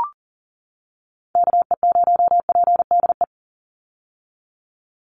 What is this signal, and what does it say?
Morse code, heard as a keyed tone, about 40 wpm, 700 Hz. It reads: KE0PDE
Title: USA Call Signs 40wpm